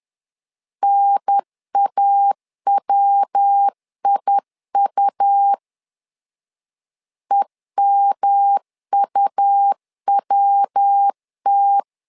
Morse code, heard as a keyed tone, about 10 wpm, 800 Hz. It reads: NAWIU EMUWT
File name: LEZIONI / 4 Q1 10wpm 2448